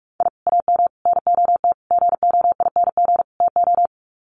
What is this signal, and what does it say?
Morse code, heard as a keyed tone, about 44 wpm, 700 Hz. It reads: I AM NOT GOING TO
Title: Sets of 5 Words Encore 45wpm